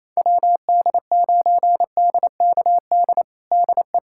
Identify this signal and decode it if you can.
Morse code, heard as a keyed tone, about 28 wpm, 700 Hz. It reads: WD9DKD DE
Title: USA Callsign DE Callsign 28wpm